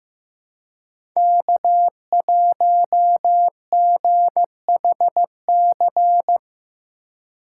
Morse code, heard as a keyed tone, about 18 wpm, 700 Hz. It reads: K1GHC
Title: Speed Racing - USA 1x3 Call Signs 15wpm